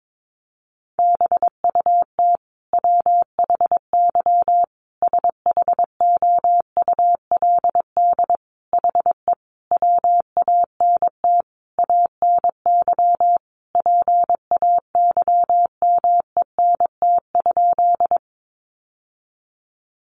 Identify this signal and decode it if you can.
Morse code, heard as a keyed tone, about 22 wpm, 700 Hz. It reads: BUT WHY SHOULD HE WANT ANY PAYMENT?
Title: Sentences from Top 1200 Words 22wpm